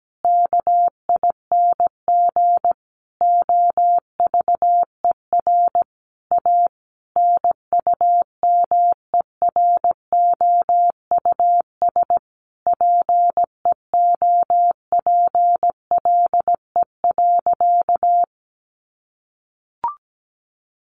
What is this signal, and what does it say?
Morse code, heard as a keyed tone, about 17 wpm, 700 Hz. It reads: KING OVER A NUMEROUS PEOPLE.
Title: Sentences from Top 1400 Words 17wpm